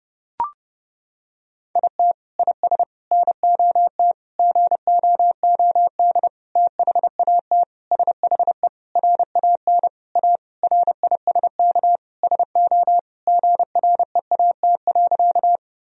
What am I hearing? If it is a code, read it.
Morse code, heard as a keyed tone, about 30 wpm, 700 Hz. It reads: IT IS NOT GOOD THAT SHE RAN A RISK SO GREAT.
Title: Sentences from Top 1000 Words - Review Part 4 - 30wpm